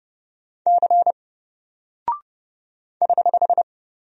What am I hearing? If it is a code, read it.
Morse code, heard as a keyed tone, about 30 wpm, 700 Hz. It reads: C <HH>
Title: Single Character (Letter:Number:Punctuation:Prosign) Alt 30wpm